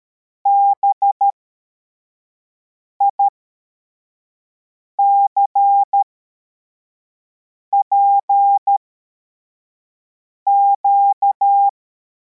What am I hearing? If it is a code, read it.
Morse code, heard as a keyed tone, about 13 wpm, spaced as for 4 wpm, 800 Hz. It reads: BICPQ
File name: formation / cw / Text71